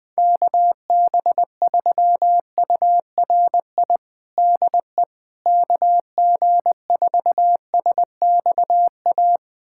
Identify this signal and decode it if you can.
Morse code, heard as a keyed tone, about 20 wpm, 700 Hz. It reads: KB3URI DE KG4SXA
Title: USA Callsign DE Callsign 20wpm